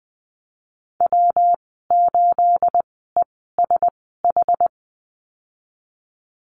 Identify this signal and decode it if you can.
Morse code, heard as a keyed tone, about 20 wpm, spaced as for 15 wpm, 700 Hz. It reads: W8ESH
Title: 219 Lesson 35g - Number 0 - Call Signs 20wpm